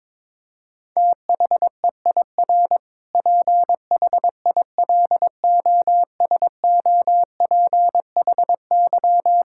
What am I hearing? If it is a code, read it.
Morse code, heard as a keyed tone, about 21 wpm, 700 Hz. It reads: THEIR PHILOSOPHY
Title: Sentences from Top 1000 Words - Review Part 6 - 22wpm